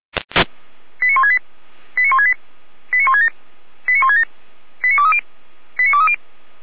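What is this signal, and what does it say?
Five-tone selective call (ZVEI), heard as quick runs of five tones, tones of about 80 ms. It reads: 89168 89168 89167 89167 89230 89230
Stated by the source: ZVEI five-tone call received on an SDR, from feuerwehr-orlamuende.org.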